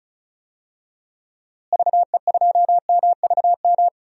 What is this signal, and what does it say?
Morse code, heard as a keyed tone, about 36 wpm, 700 Hz. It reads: VE2MVM
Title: DXCC-100 Call Signs - 6 Characters - Repeated 2x - 35wpm